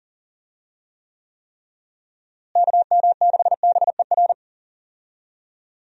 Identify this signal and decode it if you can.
Morse code, heard as a keyed tone, about 40 wpm, 700 Hz. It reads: KM6BER
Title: USA Call Signs - 6 Characters - Repeated 3x - 40wpm